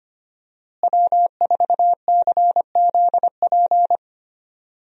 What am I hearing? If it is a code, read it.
Morse code, heard as a keyed tone, about 25 wpm, 700 Hz. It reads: W4CZP
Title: USA Call Signs - 5 Characters - Repeated 3x - 25wpm